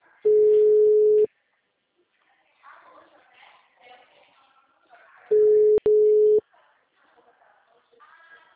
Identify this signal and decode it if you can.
Ringback tone (Europe, 425 Hz), with speech.